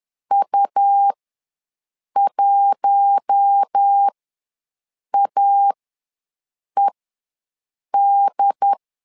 Morse code, heard as a keyed tone, about 10 wpm, spaced as for 6 wpm, 800 Hz. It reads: U1AED
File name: LEZIONI / 4 Q1 ws 10wpm 2538